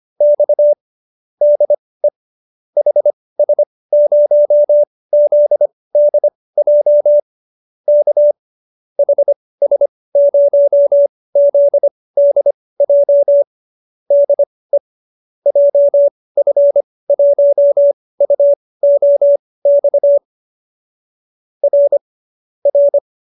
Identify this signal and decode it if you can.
Morse code, heard as a keyed tone, about 25 wpm, spaced as for 18 wpm, 570 Hz. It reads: X DE HS0ZDJ K HS0ZDJ DE JF1UOX R R